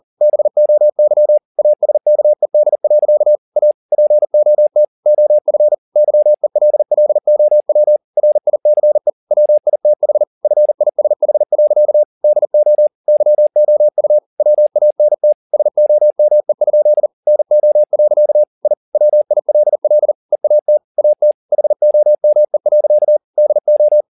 Morse code, heard as a keyed tone, about 40 wpm, 600 Hz. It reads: BOY ASKED. A POT OF YELLOW RICE WITH FISH. DO YOU WANT SOME? NO. I WILL EAT AT HOME. DO